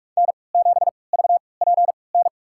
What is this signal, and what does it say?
Morse code, heard as a keyed tone, about 45 wpm, spaced as for 25 wpm, 700 Hz. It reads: N7VPN
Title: USA Call Signs - 5 Characters - 3x Character Spacing - 45wpm